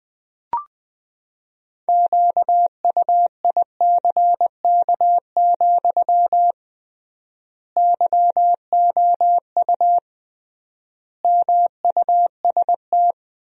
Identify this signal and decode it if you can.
Morse code, heard as a keyed tone, about 20 wpm, 700 Hz. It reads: QUICK, YOU MUST
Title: 3x Word-Spacing - Sentences from Top 500 Words - 20wpm